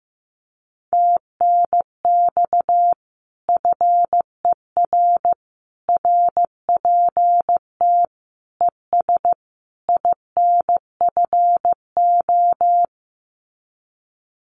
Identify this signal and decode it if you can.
Morse code, heard as a keyed tone, about 15 wpm, 700 Hz. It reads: TNX FER RPT ES INFO
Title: Common QSO Phrases Encore 15wpm